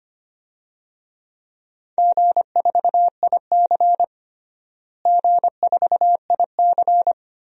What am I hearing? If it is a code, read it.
Morse code, heard as a keyed tone, about 25 wpm, 700 Hz. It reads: G4IC G4IC
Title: DXCC-100 Call Signs - 4 Characters - Repeated 2x - 25wpm